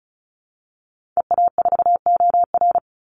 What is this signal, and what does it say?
Morse code, heard as a keyed tone, about 34 wpm, 700 Hz. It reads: EA4OR
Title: Call Signs Repeated 2x - 35wpm